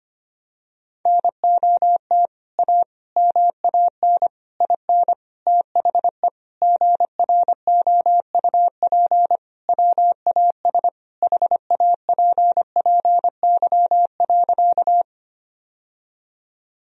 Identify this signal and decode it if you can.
Morse code, heard as a keyed tone, about 25 wpm, 700 Hz. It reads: NOT A MAN IN THE GROUP WAS HAPPY.
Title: Sentences from Top 1000 Words - Review Part 1 - 25wpm